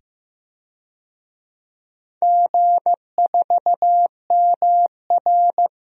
Morse code, heard as a keyed tone, about 16 wpm, 700 Hz. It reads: G4MR
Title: DXCC-100 Call Signs - 4 Characters - 15wpm